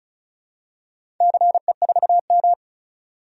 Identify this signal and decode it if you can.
Morse code, heard as a keyed tone, about 34 wpm, 700 Hz. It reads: CE4M
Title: DXCC-100 Call Signs - 4 Characters - 35wpm